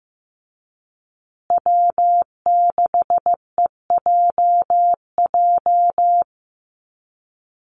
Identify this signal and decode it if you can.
Morse code, heard as a keyed tone, about 15 wpm, 700 Hz. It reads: W6EJJ
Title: USA Call Signs Repeated 3x 15wpm